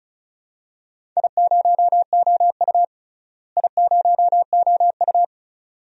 Morse code, heard as a keyed tone, about 36 wpm, 700 Hz. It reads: I0OU I0OU
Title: DXCC-100 Call Signs - 4 Characters - Repeated 2x - 35wpm